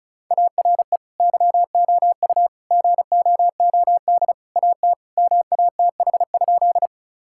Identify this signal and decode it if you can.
Morse code, heard as a keyed tone, about 36 wpm, 700 Hz. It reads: ARE YOU GOOD AT MATH?
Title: Sentences from Top 1000 Words - Review Part 6 - 35wpm